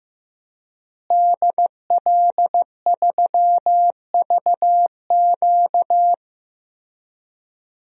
Morse code, heard as a keyed tone, about 15 wpm, 700 Hz. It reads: DL3VQ
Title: DXCC-100 Call Signs - 5 Characters - Repeated 2x - 15wpm